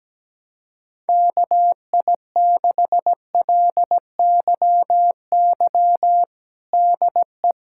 Morse code, heard as a keyed tone, about 17 wpm, 700 Hz. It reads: KI6LYY DE
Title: USA Callsign DE Callsign - 6 Characters - 17wpm